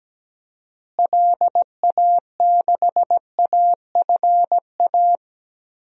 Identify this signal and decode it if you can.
Morse code, heard as a keyed tone, about 17 wpm, 700 Hz. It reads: LA6AFA
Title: DXCC-100 Call Signs - 6 Characters - Repeated 2x - 17wpm